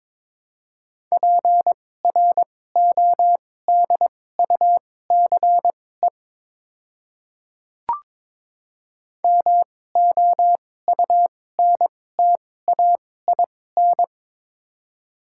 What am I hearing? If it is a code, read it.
Morse code, heard as a keyed tone, about 21 wpm, spaced as for 16 wpm, 700 Hz. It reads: PRODUCE MOUNTAIN
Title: 2x Character Spacing - Top 500 Words - 22wpm